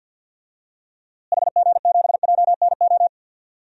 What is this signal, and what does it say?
Morse code, heard as a keyed tone, about 52 wpm, 700 Hz. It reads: HG7JNO
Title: DXCC-100 Call Signs - 6 Characters - Repeated 2x - 50wpm